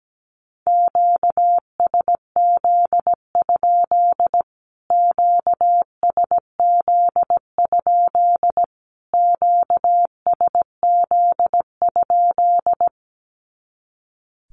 Morse code, heard as a keyed tone, about 17 wpm, 700 Hz. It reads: QSZ? QSZ? QSZ?
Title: Q-Codes Repeated 3x 17wpm